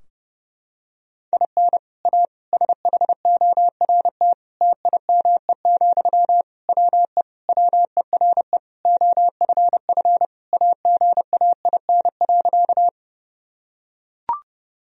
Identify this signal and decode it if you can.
Morse code, heard as a keyed tone, about 32 wpm, 700 Hz. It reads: IN A SHORT TIME, WE WERE OFF AGAIN.
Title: Sentences from Top 1000 Words - Review Part 2 - 30wpm